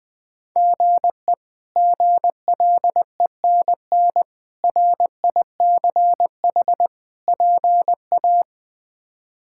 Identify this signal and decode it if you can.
Morse code, heard as a keyed tone, about 20 wpm, 700 Hz. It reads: GE GLENN RICH PA
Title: K1USN SST QSO Practice - 20wpm